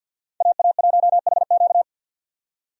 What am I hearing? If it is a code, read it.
Morse code, heard as a keyed tone, about 52 wpm, 700 Hz. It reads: AA1HQ
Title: USA Call Signs - 5 Characters - 50wpm